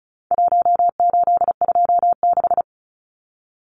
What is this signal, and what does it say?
Morse code, heard as a keyed tone, about 34 wpm, 700 Hz. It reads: 1826
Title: Numbers - Contest Style - 4-Digit 35wpm